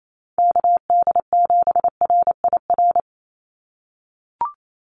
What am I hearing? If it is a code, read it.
Morse code, heard as a keyed tone, about 28 wpm, 700 Hz. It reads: KD7RIR
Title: USA Call Signs 28wpm